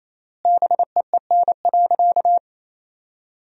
Morse code, heard as a keyed tone, about 28 wpm, 700 Hz. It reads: BEEN.
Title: 3x Word-Spacing - Sentences from Top 300 Words - 28wpm